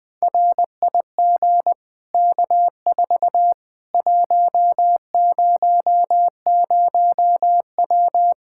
Morse code, heard as a keyed tone, about 20 wpm, 700 Hz. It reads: RIG K4 100W
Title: Practice QSO - Exchange 2 - Info - 20wpm